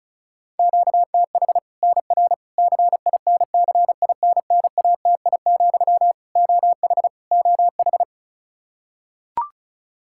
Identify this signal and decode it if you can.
Morse code, heard as a keyed tone, about 36 wpm, 700 Hz. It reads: QTH NR CINCINNATI, OH OH
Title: Practice QSO - Exchange 1 - Beginning - 35wpm